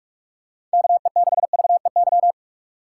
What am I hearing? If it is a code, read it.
Morse code, heard as a keyed tone, about 41 wpm, 700 Hz. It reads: KE6VEY
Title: USA Call Signs - 6 Characters - 6x Character Spacing - 45wpm